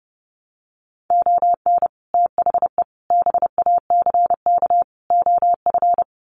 Morse code, heard as a keyed tone, about 30 wpm, 700 Hz. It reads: ON THE BACK OF
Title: Sets of 4 Words 30wpm